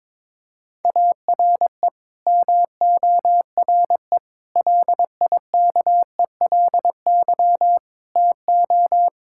Morse code, heard as a keyed tone, about 22 wpm, 700 Hz. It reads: ARE MORE LIKELY TO
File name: practice / 22wpm / Build A Phrase - 5-Word Phrases - 22wpm